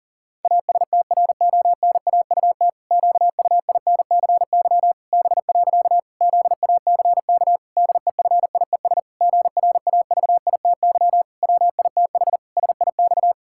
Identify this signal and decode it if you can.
Morse code, heard as a keyed tone, about 40 wpm, 700 Hz. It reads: ASTRONAUT QUINCY B. ZACK DEFIES GRAVITY WITH SIX
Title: Pangram Challenge v2 - 40wpm